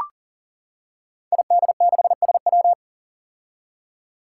Morse code, heard as a keyed tone, about 40 wpm, 700 Hz. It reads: ID6SW
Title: Five Challenging Letter-Number Combinations 40wpm